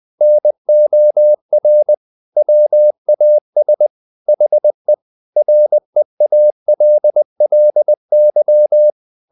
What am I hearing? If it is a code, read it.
Morse code, heard as a keyed tone, about 20 wpm, 600 Hz. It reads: NOR WAS HE REALLY
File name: the old man and the sea part 2 20wpm